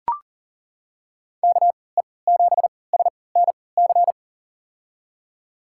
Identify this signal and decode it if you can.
Morse code, heard as a keyed tone, about 40 wpm, spaced as for 23 wpm, 700 Hz. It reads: KE7SNC
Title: USA Call Signs - 6 Characters - 3x Character Spacing - 40wpm